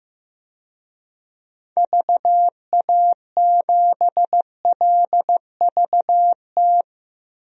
Morse code, heard as a keyed tone, about 15 wpm, 700 Hz. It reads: VA7LVT
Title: DXCC-100 Call Signs - 6 Characters - Repeated 3x - 15wpm